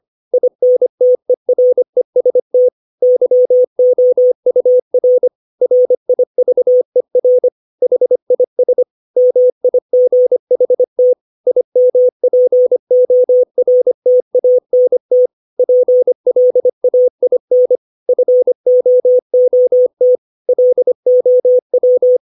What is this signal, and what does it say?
Morse code, heard as a keyed tone, about 25 wpm, 500 Hz. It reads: INTEREST YOUR RIVER HIS MIGHT IMPORTANT PLAIN FOOT LOW
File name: 632 words / 632 Words 25WPM